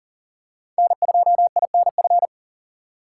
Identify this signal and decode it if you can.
Morse code, heard as a keyed tone, about 40 wpm, 700 Hz. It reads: N2INF
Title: USA Call Signs - 5 Characters - 4x Character Spacing - 40wpm